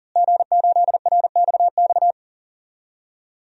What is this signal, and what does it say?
Morse code, heard as a keyed tone, about 40 wpm, 700 Hz. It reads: G8RXX
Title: DXCC-100 Call Signs - 5 Characters - Repeated 2x - 40wpm